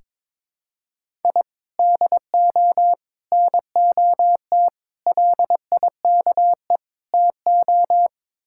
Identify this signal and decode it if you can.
Morse code, heard as a keyed tone, about 22 wpm, 700 Hz. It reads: I DO NOT LIKE TO
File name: practice / 22wpm / Build A Phrase - 5-Word Phrases - 22wpm